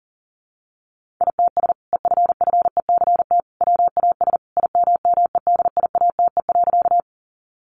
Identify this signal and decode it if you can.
Morse code, heard as a keyed tone, about 40 wpm, 700 Hz. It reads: ITS EFFECT WAS IMMEDIATE.
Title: Sentences from Top 1200 Words 40wpm